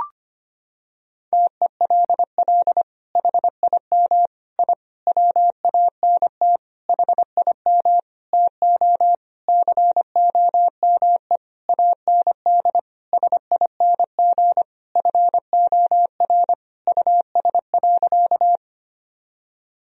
Morse code, heard as a keyed tone, about 25 wpm, 700 Hz. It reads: TELL HIM I WANT HIM TO COME AND SING FOR US.
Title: Sentences from Top 1000 Words - Review Part 2 - 25wpm